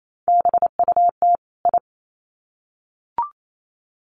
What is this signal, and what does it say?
Morse code, heard as a keyed tone, about 28 wpm, 700 Hz. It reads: BUT I
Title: Sets of 2 Words 28wpm